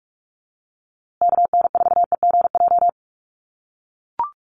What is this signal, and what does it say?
Morse code, heard as a keyed tone, about 44 wpm, 690 Hz. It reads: KN4EGJ
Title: USA Call Signs Repeated 3x 45wpm